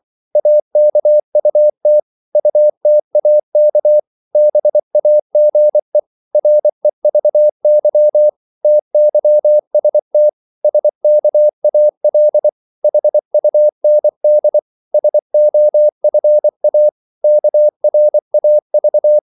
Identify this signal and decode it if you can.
Morse code, heard as a keyed tone, about 24 wpm, 600 Hz. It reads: AKUT UTAK BAGE REVY TYST SKAL HUND SOFA KRAV
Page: Korte danske ord 24wpm